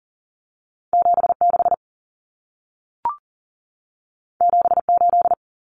Morse code, heard as a keyed tone, about 40 wpm, 700 Hz. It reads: 76 78
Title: Numbers - 2-Digit 40wpm